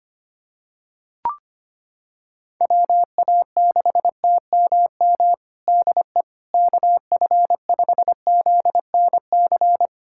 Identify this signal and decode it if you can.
Morse code, heard as a keyed tone, about 25 wpm, 700 Hz. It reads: WA6TMM DE KF5ZNC
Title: USA Callsign DE Callsign - 6 Characters - 25wpm